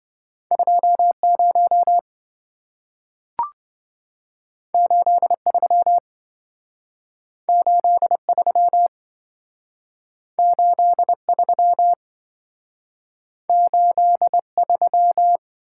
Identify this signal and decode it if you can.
Morse code, heard as a keyed tone, about 27 wpm, 700 Hz. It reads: 20 83 83 83 83